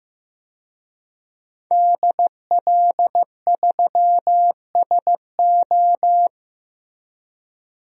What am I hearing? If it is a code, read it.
Morse code, heard as a keyed tone, about 16 wpm, 700 Hz. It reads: DL3SO
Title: DXCC-100 Call Signs - 5 Characters - Repeated 3x - 15wpm